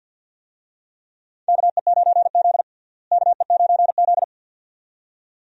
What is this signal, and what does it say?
Morse code, heard as a keyed tone, about 49 wpm, 700 Hz. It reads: KE9Z KE9Z
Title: USA Call Signs - 4 Characters - Repeated 2x - 50wpm